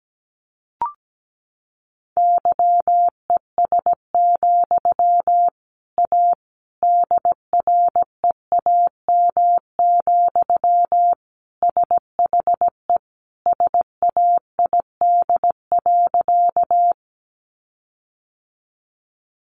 Morse code, heard as a keyed tone, about 17 wpm, 700 Hz. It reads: YES, A DREAM, SHE SAID.
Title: Sentences from Top 1500 Words 17wpm